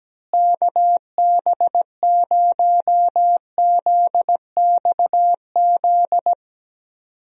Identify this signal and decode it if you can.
Morse code, heard as a keyed tone, about 17 wpm, 700 Hz. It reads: KB0ZXZ
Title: USA Call Signs - 6 Characters - Repeated 3x - 17wpm